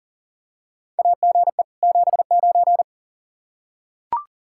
Morse code, heard as a keyed tone, about 40 wpm, 700 Hz. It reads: AGE 79
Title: K1USN SST QSO Practice 40wpm